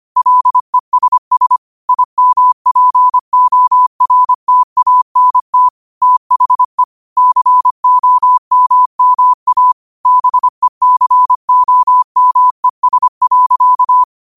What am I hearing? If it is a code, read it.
Morse code, heard as a keyed tone, about 25 wpm, 1000 Hz. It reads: LESS IMPORTANT THE COMMA BECOMES.
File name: qotdfiles / 25wpm